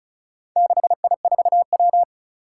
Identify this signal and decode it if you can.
Morse code, heard as a keyed tone, about 36 wpm, 700 Hz. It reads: 6I4W
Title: Four Challenging Letter-Number Combinations 35wpm